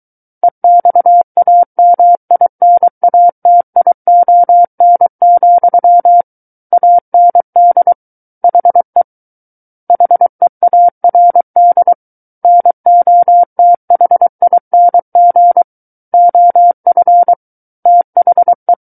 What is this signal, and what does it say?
Morse code, heard as a keyed tone, about 23 wpm, 700 Hz. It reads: EXAMINATION, AND HE HEARD NOTHING OF THE
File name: Book One / 14-Chapter 14 - 23 WPM